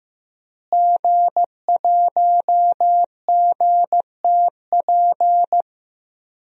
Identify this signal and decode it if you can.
Morse code, heard as a keyed tone, about 15 wpm, 700 Hz. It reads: G1GTP
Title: DXCC-100 Call Signs - 5 Characters - Repeated 3x - 15wpm